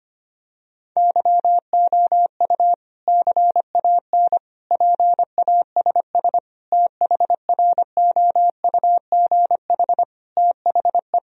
Morse code, heard as a keyed tone, about 25 wpm, 700 Hz. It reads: YOU CAN PASS THROUGH THE
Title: Sentences from Top 1000 Words - Review Part 1 - 25wpm